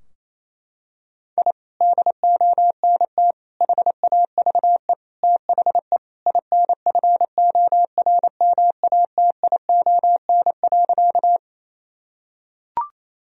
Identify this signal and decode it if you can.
Morse code, heard as a keyed tone, about 28 wpm, 700 Hz. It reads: I DONT HAVE THE INFORMATION.
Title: Sentences from Top 1000 Words - Review Part 3 - 28wpm